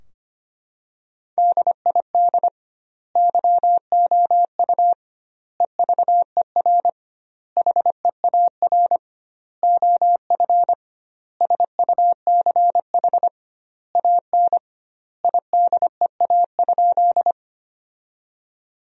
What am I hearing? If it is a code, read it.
Morse code, heard as a keyed tone, about 25 wpm, 700 Hz. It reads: DID YOU EVER HEAR OF SUCH AN IDEA?
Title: 2x Word-Spacing - Sentences from Top 400 Words - 25wpm